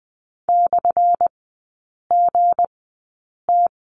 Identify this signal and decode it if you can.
Morse code, heard as a keyed tone, about 20 wpm, spaced as for 8 wpm, 700 Hz. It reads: /GT
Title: 178 Lesson 29e - Slash - Three Characters 20wpm